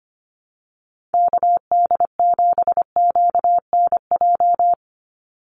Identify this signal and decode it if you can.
Morse code, heard as a keyed tone, about 25 wpm, 700 Hz. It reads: KD7QNJ
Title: USA Call Signs 25wpm